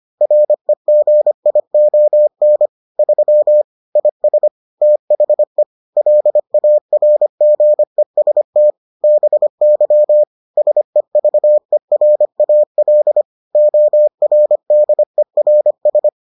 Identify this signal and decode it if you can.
Morse code, heard as a keyed tone, about 25 wpm, 600 Hz. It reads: REGION 3 IS THE LARGEST BY SEVERAL ORDERS